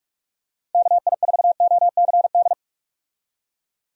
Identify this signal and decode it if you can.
Morse code, heard as a keyed tone, about 44 wpm, 700 Hz. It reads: KI4OCD
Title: USA Call Signs - 6 Characters - 45wpm